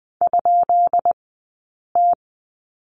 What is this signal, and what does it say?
Morse code, heard as a keyed tone, about 19 wpm, spaced as for 8 wpm, 700 Hz. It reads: ?T
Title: 151 Lesson 25d - Letter p - Two Characters 20wpm